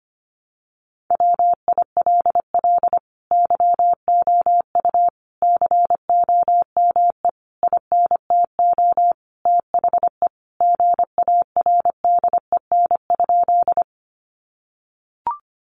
Morse code, heard as a keyed tone, about 25 wpm, 700 Hz. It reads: WILL YOU COME INTO THE GARDEN?
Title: Sentences from Top 700 Words 25wpm